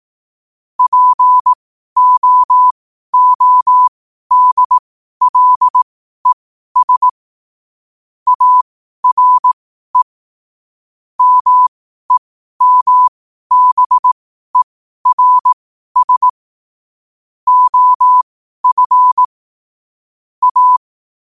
Morse code, heard as a keyed tone, about 18 wpm, spaced as for 12 wpm, 1000 Hz. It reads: POODLES ARE MEMBERS OF A
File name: qotdfiles / 12wpm